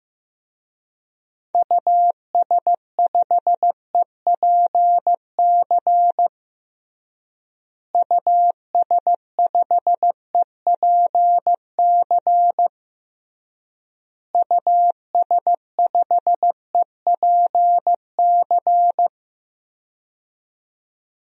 Morse code, heard as a keyed tone, about 15 wpm, 700 Hz. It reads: US5EPC US5EPC US5EPC
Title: DXCC-100 Call Signs - 6 Characters - Repeated 3x - 15wpm